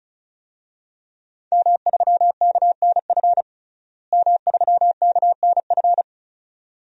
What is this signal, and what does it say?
Morse code, heard as a keyed tone, about 34 wpm, 700 Hz. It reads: M3KNF M3KNF
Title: DXCC-100 Call Signs - 5 Characters - Repeated 2x - 35wpm